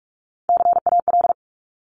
Morse code, heard as a keyed tone, about 46 wpm, 700 Hz. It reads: CAL
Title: Top Three Letter Combinations in Words 45wpm